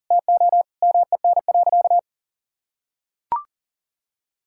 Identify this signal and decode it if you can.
Morse code, heard as a keyed tone, about 40 wpm, 700 Hz. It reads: TO MEN.
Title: Sentences from Top 1000 Words - Review Part 6 - 40wpm